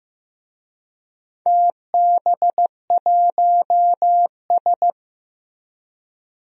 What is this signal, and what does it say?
Morse code, heard as a keyed tone, about 15 wpm, 700 Hz. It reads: TB1S
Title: DXCC-100 Call Signs - 4 Characters - Repeated 3x - 15wpm